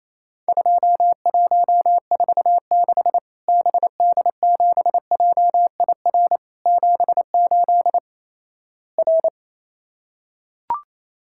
Morse code, heard as a keyed tone, about 28 wpm, 700 Hz. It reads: 2146 BD7JIR 78 R
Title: Worked All Europe DX Contest - QTC Exchange - 28wpm